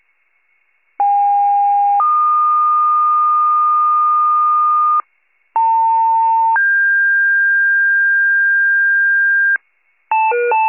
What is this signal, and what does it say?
Two-tone page, heard as a one-second tone, then a three-second tone, about 810 then 1220 Hz, about 900 then 1600 Hz, with noise. It is heard from 1.0 to 9.6 s.